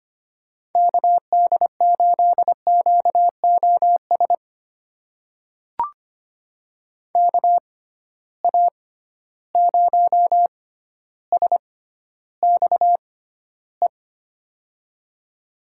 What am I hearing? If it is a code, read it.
Morse code, heard as a keyed tone, about 25 wpm, spaced as for 13 wpm, 700 Hz. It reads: KD8QOS KA0SXE
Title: USA Call Signs - 6 Characters - 6x Character Spacing - 25wpm